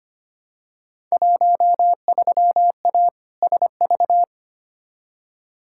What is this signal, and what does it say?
Morse code, heard as a keyed tone, about 25 wpm, 700 Hz. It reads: 13A SV
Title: Field Day Exchanges Encore - 25wpm